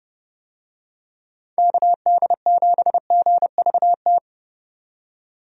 Morse code, heard as a keyed tone, about 30 wpm, 700 Hz. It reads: KD7GVT
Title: USA Call Signs - 6 Characters - 30wpm